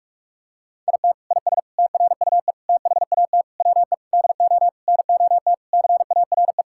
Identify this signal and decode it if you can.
Morse code, heard as a keyed tone, about 46 wpm, 700 Hz. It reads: IT IS TRUE THAT WE DO NOT CARE
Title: Sentences from Top 1600 Words 45wpm